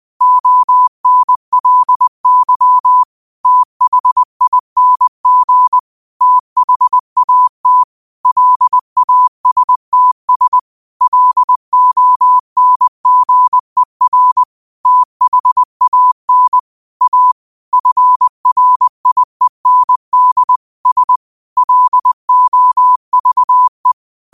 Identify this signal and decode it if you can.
Morse code, heard as a keyed tone, about 20 wpm, 1000 Hz. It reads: ONLY THING THAT LASTS LONGER THAN A FRIEND S LOVE
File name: qotdfiles / 20wpm